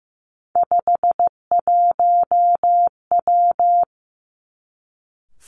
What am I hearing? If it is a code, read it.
Morse code, heard as a keyed tone, about 15 wpm, 700 Hz. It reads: 51W
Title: Three Challenging Letter-Number Combinations 15wpm